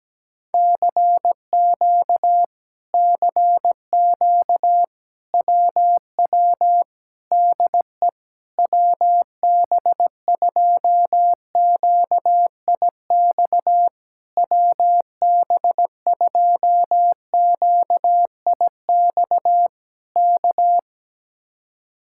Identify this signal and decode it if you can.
Morse code, heard as a keyed tone, about 17 wpm, 700 Hz. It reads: CQ CQ WW DE WB2QIX WB2QIX K
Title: Speed-Racing Common QSO Phrases Encore - 17wpm